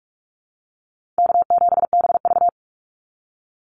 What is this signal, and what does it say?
Morse code, heard as a keyed tone, about 44 wpm, 700 Hz. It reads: K7BV
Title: Easy Call Signs Repeated 3x - 45wpm